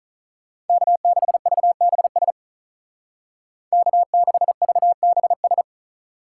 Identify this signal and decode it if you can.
Morse code, heard as a keyed tone, about 45 wpm, 700 Hz. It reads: K6VBS K6VBS